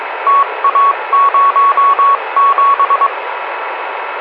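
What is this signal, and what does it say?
Morse code, heard as a keyed tone, about 22 wpm, 1120 Hz, with radio noise. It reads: TA07
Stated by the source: Morse signal on 144.375 from station RU4PG, locator LO45OS, RDA district TA07.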